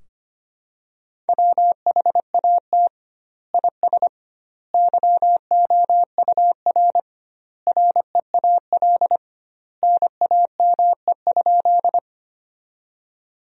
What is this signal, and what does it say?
Morse code, heard as a keyed tone, about 25 wpm, 700 Hz. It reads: WHAT IS YOUR REAL NAME?
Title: 2x Word-Spacing - Sentences from Top 300 Words - 25wpm